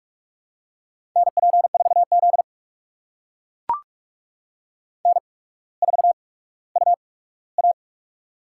Morse code, heard as a keyed tone, about 44 wpm, spaced as for 19 wpm, 690 Hz. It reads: NP4Z N4UA
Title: USA Call Signs - 4 Characters - 8x Character Spacing - 45wpm